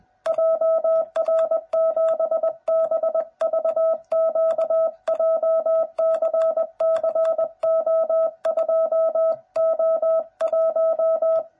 Morse code, heard as a keyed tone, about 20 wpm, 660 Hz. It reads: JR7BVQ JCCO2O1
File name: jh8cbh / 2021cw / cw09b